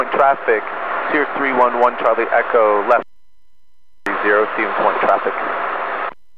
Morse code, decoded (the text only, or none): none